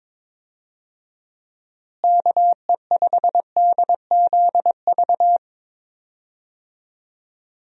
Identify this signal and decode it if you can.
Morse code, heard as a keyed tone, about 22 wpm, 700 Hz. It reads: KE5DZV
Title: USA Call Signs - 6 Characters - Repeated 2x - 22wpm